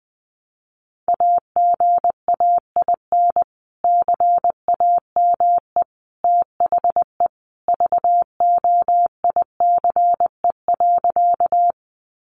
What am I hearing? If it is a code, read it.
Morse code, heard as a keyed tone, about 20 wpm, 700 Hz. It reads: AGAIN CAME THE VOICE.
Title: Sentences from Top 500 Words 20wpm